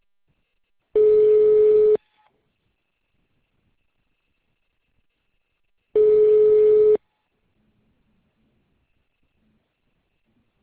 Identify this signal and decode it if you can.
Ringback tone (Europe, 425 Hz).